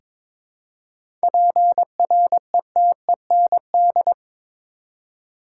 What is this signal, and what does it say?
Morse code, heard as a keyed tone, about 22 wpm, 700 Hz. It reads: PRETEND
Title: Seven Letter Words - 22wpm